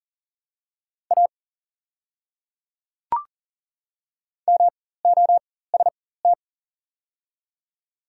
Morse code, heard as a keyed tone, about 42 wpm, spaced as for 19 wpm, 700 Hz. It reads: A MOST
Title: 4x Character Spacing - Top 100 Words - 40wpm